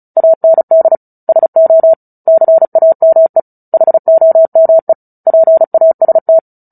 Morse code, heard as a keyed tone, about 34 wpm, 650 Hz. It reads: AND SO CAME HOME PAST
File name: Book Two / 09-Chapter 9 - 35 WPM